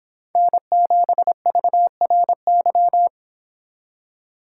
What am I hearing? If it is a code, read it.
Morse code, heard as a keyed tone, about 26 wpm, 700 Hz. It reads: N7VRY